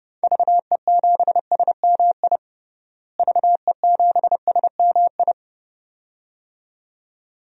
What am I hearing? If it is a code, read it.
Morse code, heard as a keyed tone, about 32 wpm, 700 Hz. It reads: VE7SMI VE7SMI
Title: DXCC-100 Call Signs - 6 Characters - Repeated 2x - 30wpm